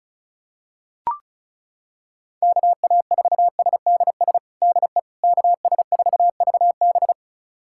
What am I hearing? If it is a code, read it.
Morse code, heard as a keyed tone, about 36 wpm, 700 Hz. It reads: KA4SDS DE KS4VB
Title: USA Callsign DE Callsign 35wpm